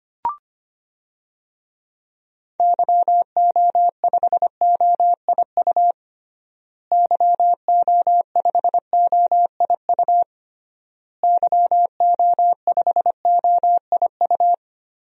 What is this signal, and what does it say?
Morse code, heard as a keyed tone, about 25 wpm, 700 Hz. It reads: YO5OIU YO5OIU YO5OIU
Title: DXCC-100 Call Signs - 6 Characters - Repeated 3x - 25wpm